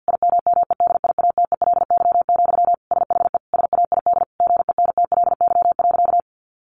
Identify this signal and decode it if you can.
Morse code, heard as a keyed tone, about 50 wpm, 700 Hz. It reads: IMMEDIATELY, SHE SAID GENTLY.
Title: Sentences from Top 1500 Words 50wpm